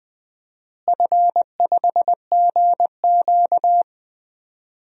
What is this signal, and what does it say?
Morse code, heard as a keyed tone, about 20 wpm, 700 Hz. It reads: F5GQ
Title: DXCC-100 Call Signs - 4 Characters - Repeated 3x - 20wpm